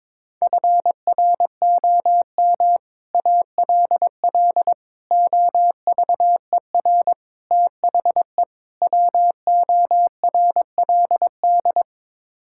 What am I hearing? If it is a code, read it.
Morse code, heard as a keyed tone, about 22 wpm, 700 Hz. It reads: FROM ALL OVER THE WORLD
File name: practice / 22wpm / Build A Phrase - 5-Word Phrases - 22wpm